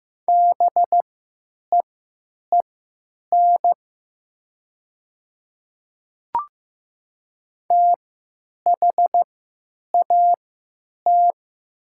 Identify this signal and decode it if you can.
Morse code, heard as a keyed tone, about 15 wpm, spaced as for 9 wpm, 700 Hz. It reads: BEEN THAT
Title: 3x Character Spacing - Top 100 Words - 15wpm